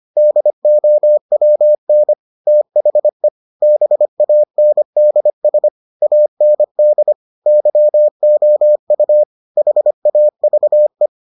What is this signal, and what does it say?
Morse code, heard as a keyed tone, about 25 wpm, 600 Hz. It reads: DOWN THE BANDS AND YOU HAVE